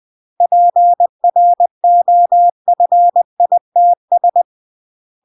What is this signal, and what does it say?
Morse code, heard as a keyed tone, about 20 wpm, 700 Hz. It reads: PROFITS